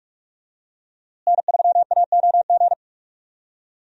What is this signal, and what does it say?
Morse code, heard as a keyed tone, about 46 wpm, 700 Hz. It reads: N3AOG
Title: USA Call Signs - 5 Characters - Repeated 2x - 45wpm